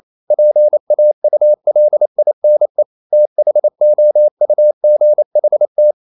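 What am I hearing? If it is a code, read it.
Morse code, heard as a keyed tone, about 28 wpm, 600 Hz. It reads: PAULINE THOUGHT
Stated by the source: Morse code practice at 28 wpm, text of Ken Follett's novel Never in English, from CW med Gnister.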